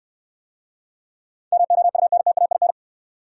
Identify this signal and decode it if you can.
Morse code, heard as a keyed tone, about 68 wpm, 700 Hz. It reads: MOUNTAIN